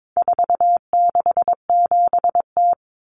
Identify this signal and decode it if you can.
Morse code, heard as a keyed tone, about 22 wpm, 700 Hz. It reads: 467T
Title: Common Cut Numbers - 4-Digit 22wpm